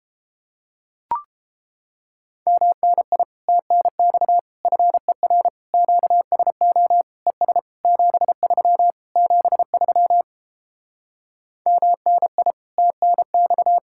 Morse code, heard as a keyed tone, about 29 wpm, 700 Hz. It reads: MNI TNX FER QSO ES 73 73 MNI TNX